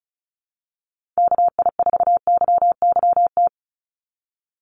Morse code, heard as a keyed tone, about 36 wpm, 700 Hz. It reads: KI4YYT
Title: USA Call Signs 35wpm